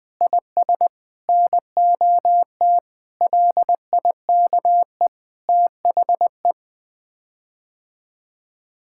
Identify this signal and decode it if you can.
Morse code, heard as a keyed tone, about 21 wpm, 700 Hz. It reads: IS NOT LIKE THE
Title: Sets of 8 Words - 20wpm